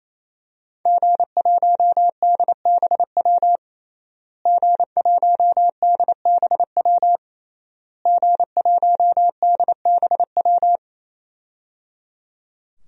Morse code, heard as a keyed tone, about 28 wpm, 700 Hz. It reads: G1DBW G1DBW G1DBW
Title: DXCC-100 Call Signs - 5 Characters - Repeated 3x - 28wpm